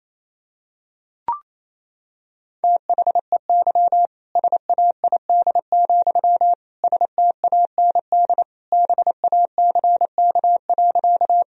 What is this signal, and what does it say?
Morse code, heard as a keyed tone, about 28 wpm, 700 Hz. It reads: THEY SAID, STAND BACK.
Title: Sentences from Top 1000 Words - Review Part 1 - 28wpm